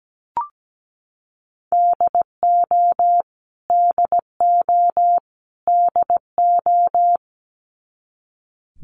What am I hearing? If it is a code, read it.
Morse code, heard as a keyed tone, about 17 wpm, 700 Hz. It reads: DO DO DO
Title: Top 200 Words Repeated 17wpm